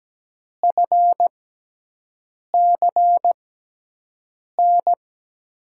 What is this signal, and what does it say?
Morse code, heard as a keyed tone, about 17 wpm, spaced as for 6 wpm, 700 Hz. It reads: FCN
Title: USA Call Signs - 6 Characters - 6x Character Spacing - 17wpm